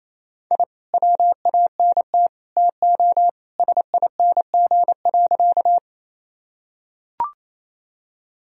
Morse code, heard as a keyed tone, about 28 wpm, 700 Hz. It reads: I WANT TO SING.
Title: Sentences from Top 1000 Words - Review Part 2 - 28wpm